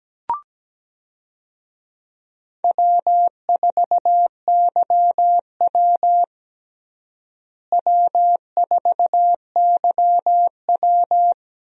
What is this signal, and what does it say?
Morse code, heard as a keyed tone, about 17 wpm, 700 Hz. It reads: W4YW W4YW
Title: USA Call Signs - 4 Characters - Repeated 2x - 17wpm